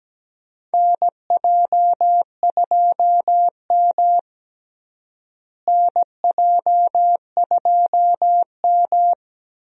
Morse code, heard as a keyed tone, about 17 wpm, 700 Hz. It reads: NJ2M NJ2M
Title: USA Call Signs - 4 Characters - Repeated 2x - 17wpm